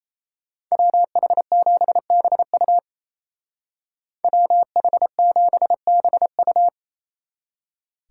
Morse code, heard as a keyed tone, about 33 wpm, 700 Hz. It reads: WH7BU WH7BU
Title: Speed Racing - USA Call Signs 28wpm